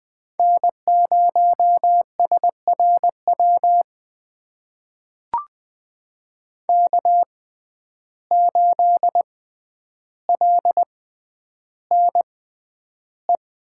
Morse code, heard as a keyed tone, about 20 wpm, spaced as for 10 wpm, 700 Hz. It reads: N0SRW K8LNE
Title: USA Call Signs - 5 Characters - 6x Character Spacing - 20wpm